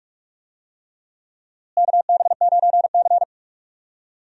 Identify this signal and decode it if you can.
Morse code, heard as a keyed tone, about 42 wpm, 700 Hz. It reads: KB9C
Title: USA Call Signs - 4 Characters - Repeated 3x - 45wpm